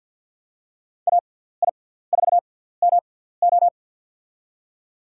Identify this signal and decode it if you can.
Morse code, heard as a keyed tone, about 52 wpm, spaced as for 17 wpm, 700 Hz. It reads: AI4MO
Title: USA Call Signs - 6x Character Spacing - 50wpm